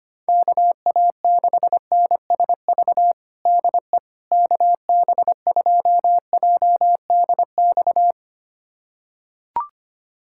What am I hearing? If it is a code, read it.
Morse code, heard as a keyed tone, about 25 wpm, 700 Hz. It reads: KA6NSV DE KB2JDX
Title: USA Callsign DE Callsign 25wpm